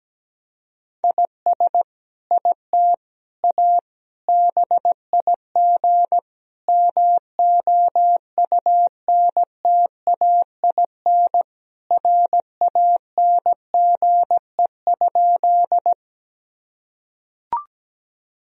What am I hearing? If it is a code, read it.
Morse code, heard as a keyed tone, about 17 wpm, 700 Hz. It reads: IS IT A BIG MOUNTAIN RANGE?
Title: Sentences from Top 1000 Words - Review Part 4 - 17wpm